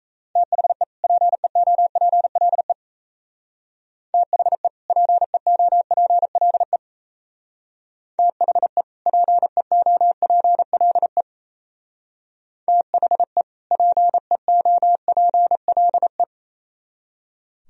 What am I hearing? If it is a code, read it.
Morse code, heard as a keyed tone, about 37 wpm, 700 Hz. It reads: THE PEOPLE THE PEOPLE THE PEOPLE THE PEOPLE